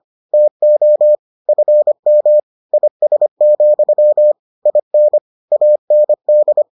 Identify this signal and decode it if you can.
Morse code, heard as a keyed tone, about 25 wpm, 600 Hz. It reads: TO FM IS, IN AND